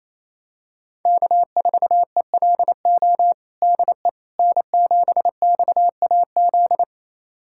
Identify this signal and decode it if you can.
Morse code, heard as a keyed tone, about 28 wpm, 700 Hz. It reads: K4ELO DE N7XAZ
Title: USA Callsign DE Callsign 28wpm